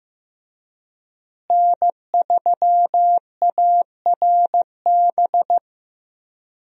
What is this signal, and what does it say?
Morse code, heard as a keyed tone, about 15 wpm, 700 Hz. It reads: N3ARB
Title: USA Call Signs - 5 Characters - Repeated 2x - 15wpm